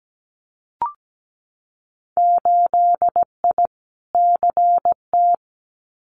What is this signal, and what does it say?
Morse code, heard as a keyed tone, about 17 wpm, 700 Hz. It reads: 8I CT
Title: Winter Field Day Exchanges Encore 17wpm